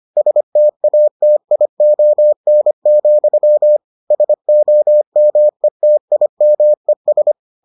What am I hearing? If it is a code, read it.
Morse code, heard as a keyed tone, about 25 wpm, 600 Hz. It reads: STATION, SOMETIMES